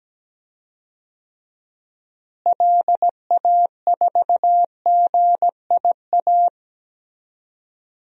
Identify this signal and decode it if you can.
Morse code, heard as a keyed tone, about 17 wpm, 700 Hz. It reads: LA4GIA
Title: DXCC-100 Call Signs - 6 Characters - Repeated 2x - 17wpm